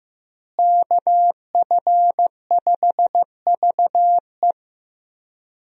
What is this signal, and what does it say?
Morse code, heard as a keyed tone, about 15 wpm, 700 Hz. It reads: KF5VE
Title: USA Call Signs - 5 Characters - Repeated 2x - 15wpm